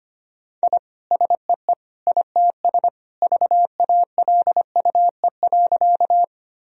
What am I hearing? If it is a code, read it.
Morse code, heard as a keyed tone, about 25 wpm, 700 Hz. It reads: I SEE ITS VALUE.
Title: Sentences from Top 1000 Words - Review Part 3 - 25wpm